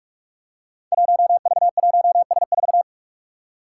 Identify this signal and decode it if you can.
Morse code, heard as a keyed tone, about 46 wpm, 700 Hz. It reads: 1V1S4
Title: Five Challenging Letter-Number Combinations 45wpm